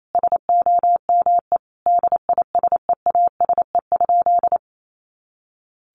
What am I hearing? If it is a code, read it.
Morse code, heard as a keyed tone, about 28 wpm, 700 Hz. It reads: SOME DISEASE?
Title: Sentences from Top 1500 Words 28wpm